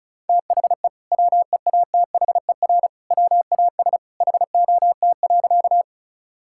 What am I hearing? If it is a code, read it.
Morse code, heard as a keyed tone, about 36 wpm, 700 Hz. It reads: THE WEATHER WAS HOT.
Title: Sentences from Top 1000 Words - Review Part 5 - 35wpm